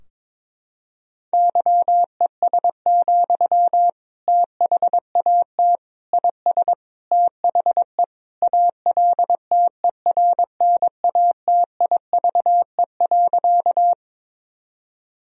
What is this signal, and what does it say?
Morse code, heard as a keyed tone, about 21 wpm, 700 Hz. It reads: YES, THAT IS THE ALTERNATIVE.
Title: Sentences from Top 1000 Words - Review Part 4 - 22wpm